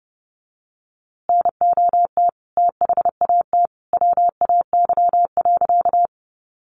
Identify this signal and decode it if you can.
Morse code, heard as a keyed tone, about 29 wpm, 700 Hz. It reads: NOT THAT WAY.
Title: A Sound Of Thunder 30wpm